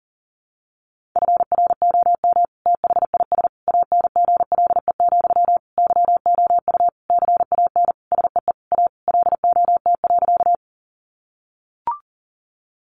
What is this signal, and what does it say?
Morse code, heard as a keyed tone, about 40 wpm, 700 Hz. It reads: FROM THIS ANGLE, YOU CAN SEE A LOT.
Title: Sentences from Top 1500 Words 40wpm